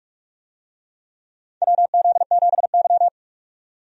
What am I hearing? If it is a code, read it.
Morse code, heard as a keyed tone, about 41 wpm, 700 Hz. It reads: WZ7Y
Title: USA Call Signs - 4 Characters - 45wpm